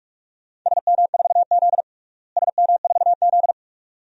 Morse code, heard as a keyed tone, about 46 wpm, 700 Hz. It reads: SM4Z SM4Z
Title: DXCC-100 Call Signs - 4 Characters - Repeated 2x - 45wpm